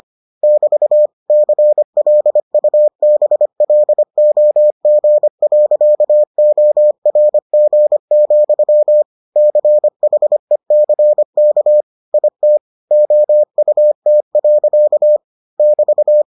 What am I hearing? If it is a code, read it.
Morse code, heard as a keyed tone, about 25 wpm, 600 Hz. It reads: <BT> CLUBLOG.ORG, CHECK IT OUT. <BT>